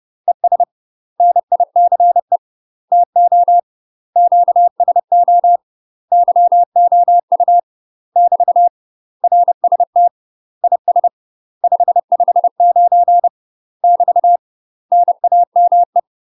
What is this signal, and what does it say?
Morse code, heard as a keyed tone, about 30 wpm, 700 Hz. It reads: ES NICE TO QSO YOU <BT> RST IS 559 <BT> NAME
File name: UK MASTER QSOs TRAINING / 30WPM / UK MASTER QSOs 0003